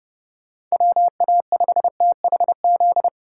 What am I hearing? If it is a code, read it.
Morse code, heard as a keyed tone, about 37 wpm, 700 Hz. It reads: WA5THZ
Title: Speed Racing - USA 2x3 Call Signs 30wpm